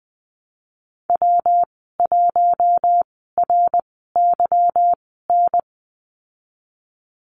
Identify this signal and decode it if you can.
Morse code, heard as a keyed tone, about 20 wpm, spaced as for 15 wpm, 700 Hz. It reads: W1RYN
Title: 147 Lesson 24g - Letter y - Call Signs 20wpm